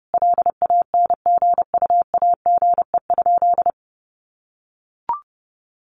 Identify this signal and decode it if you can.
Morse code, heard as a keyed tone, about 30 wpm, 700 Hz. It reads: LANGUAGE?
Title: Sentences from Top 1100 Words 30wpm